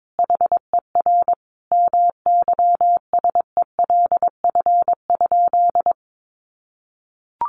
Morse code, heard as a keyed tone, about 22 wpm, 700 Hz. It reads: HER MYSELF?
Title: Sentences from Top 1300 Words 22wpm